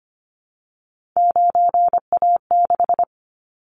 Morse code, heard as a keyed tone, about 25 wpm, 700 Hz. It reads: 9A6
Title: Callsign Prefixes for Top 25 DXCC - Repeated 3x - 25wpm